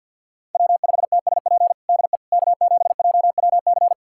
Morse code, heard as a keyed tone, about 51 wpm, 700 Hz. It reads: W5TSP DE K7JWC
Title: USA Callsign DE Callsign - 5 Characters - 50wpm